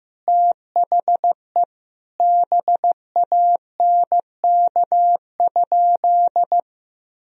Morse code, heard as a keyed tone, about 15 wpm, 700 Hz. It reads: THE BANK?
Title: Sentences from Top 1000 Words - Review Part 4 - 15wpm